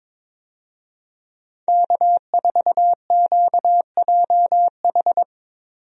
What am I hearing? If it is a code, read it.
Morse code, heard as a keyed tone, about 23 wpm, 700 Hz. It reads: K4QJH
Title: USA Call Signs - 5 Characters - 22wpm